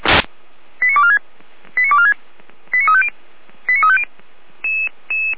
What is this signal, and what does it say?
Five-tone selective call (ZVEI), heard as quick runs of five tones, tones of about 70 ms. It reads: 89246 89246 89350 89350